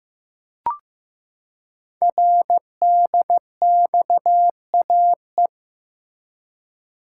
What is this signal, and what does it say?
Morse code, heard as a keyed tone, about 15 wpm, 700 Hz. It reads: RDXAE
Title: Code Groups - 5 Letters - 15wpm